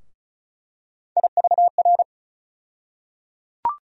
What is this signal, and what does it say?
Morse code, heard as a keyed tone, about 36 wpm, 700 Hz. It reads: IVR
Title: USA Call Sign Postfixes - 3 Letters 35wpm